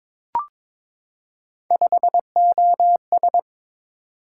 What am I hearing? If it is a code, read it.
Morse code, heard as a keyed tone, about 22 wpm, 700 Hz. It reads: 5OS
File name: practice / 22wpm / Three Letter-Number Combinations - 22wpm